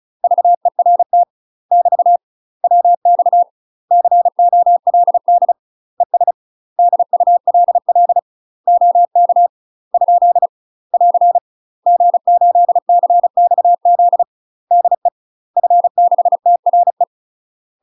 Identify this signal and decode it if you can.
Morse code, heard as a keyed tone, about 34 wpm, 700 Hz. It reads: VERT <BT> WX COLD ES DULL OK ? <AR> G8CXZ DE F6TRE
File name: UK MASTER QSOs TRAINING / 35WPM / UK MASTER QSOs 0040